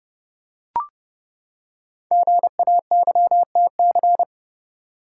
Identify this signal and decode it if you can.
Morse code, heard as a keyed tone, about 30 wpm, 700 Hz. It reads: GAYTC
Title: Code Groups - 5 Letters - 30wpm